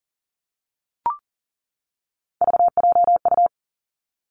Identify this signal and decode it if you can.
Morse code, heard as a keyed tone, about 40 wpm, 700 Hz. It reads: VJU
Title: Three Challenging Letter-Number Combinations 40wpm